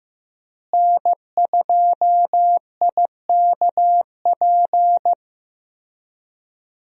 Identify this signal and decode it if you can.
Morse code, heard as a keyed tone, about 15 wpm, 700 Hz. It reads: N2IKP
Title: USA Call Signs - 5 Characters - Repeated 3x - 15wpm